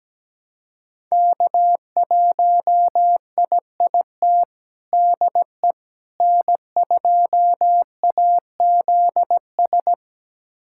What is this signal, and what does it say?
Morse code, heard as a keyed tone, about 17 wpm, 700 Hz. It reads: K1IIT DE N2AZS
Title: USA Callsign DE Callsign - 5 Characters - 17wpm